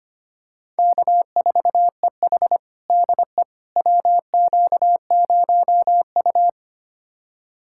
Morse code, heard as a keyed tone, about 25 wpm, 700 Hz. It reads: K4EH DE WQ0U
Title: USA Callsign DE Callsign - 4 Characters - 25wpm